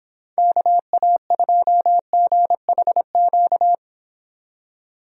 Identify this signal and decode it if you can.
Morse code, heard as a keyed tone, about 25 wpm, 700 Hz. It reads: KA2GHQ